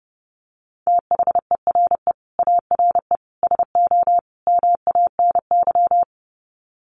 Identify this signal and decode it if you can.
Morse code, heard as a keyed tone, about 28 wpm, 700 Hz. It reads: THERE ARE SO MANY
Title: Sets of 4 Words 30wpm